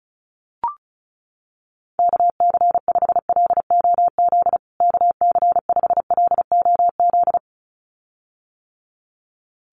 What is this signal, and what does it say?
Morse code, heard as a keyed tone, about 34 wpm, 700 Hz. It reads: KC5LOZ KC5LOZ
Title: USA Call Signs Repeated 2x 35wpm